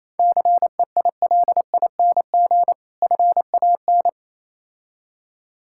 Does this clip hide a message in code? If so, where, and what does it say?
Yes: Morse code from 0.2 to 4.1 s, heard as a keyed tone, about 28 wpm, 690 Hz. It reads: CEILING FAN